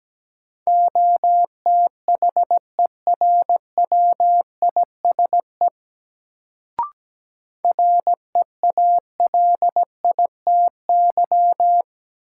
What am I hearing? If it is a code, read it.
Morse code, heard as a keyed tone, about 17 wpm, 700 Hz. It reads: OTHERWISE REALITY
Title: Sentences from Top 1200 Words - New Word Drill - 17wpm